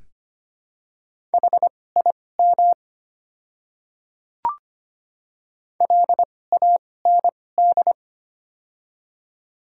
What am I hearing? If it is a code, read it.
Morse code, heard as a keyed tone, about 25 wpm, spaced as for 18 wpm, 700 Hz. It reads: HIM LAND
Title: 2x Character Spacing - Top 200 Words - 25wpm